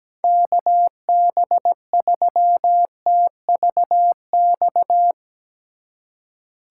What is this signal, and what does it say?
Morse code, heard as a keyed tone, about 17 wpm, 700 Hz. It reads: KB3TVX
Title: USA Call Signs - 3x Character Spacing - 17wpm